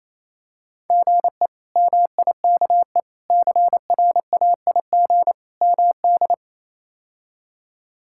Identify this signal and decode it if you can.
Morse code, heard as a keyed tone, about 28 wpm, 700 Hz. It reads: GE MIKE CRAIG MD
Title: K1USN SST QSO Practice - 28wpm